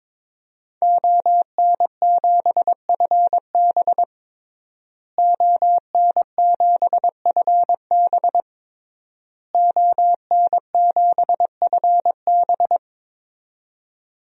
Morse code, heard as a keyed tone, about 23 wpm, 700 Hz. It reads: ON7FB ON7FB ON7FB
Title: DXCC-100 Call Signs - 5 Characters - Repeated 3x - 22wpm